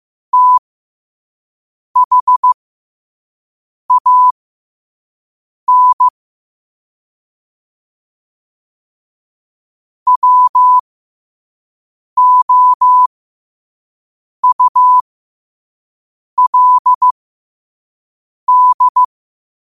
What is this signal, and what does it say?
Morse code, heard as a keyed tone, about 15 wpm, spaced as for 5 wpm, 1000 Hz. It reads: THAN WOULD
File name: qotdfiles / 5wpm